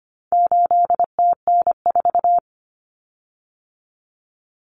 Morse code, heard as a keyed tone, about 25 wpm, 700 Hz. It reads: 8TN4
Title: Common Cut Numbers - 4-Digit 25wpm